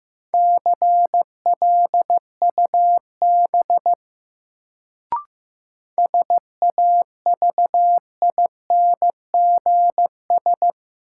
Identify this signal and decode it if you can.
Morse code, heard as a keyed tone, about 15 wpm, 700 Hz. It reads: CLUB SAVINGS
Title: Sentences from Top 1200 Words - New Word Drill - 15wpm